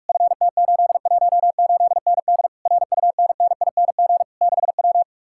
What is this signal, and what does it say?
Morse code, heard as a keyed tone, about 49 wpm, 700 Hz. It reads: FT818ND RUNNING 6W